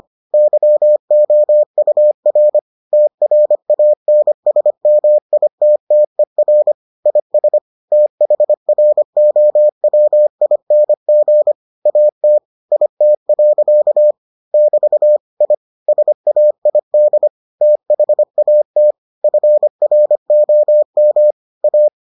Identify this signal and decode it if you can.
Morse code, heard as a keyed tone, about 25 wpm, 600 Hz. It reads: YOUR TRANSMITTER IS THROWING AT IT. <BT> I SAID THAT FROM A